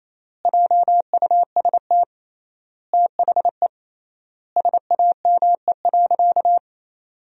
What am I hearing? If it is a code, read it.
Morse code, heard as a keyed tone, about 28 wpm, 700 Hz. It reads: JUST THE SAME.
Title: 3x Word-Spacing - Sentences from Top 300 Words - 28wpm